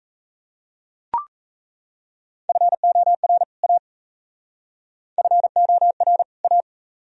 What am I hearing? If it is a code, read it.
Morse code, heard as a keyed tone, about 37 wpm, 700 Hz. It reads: FOR A FOR A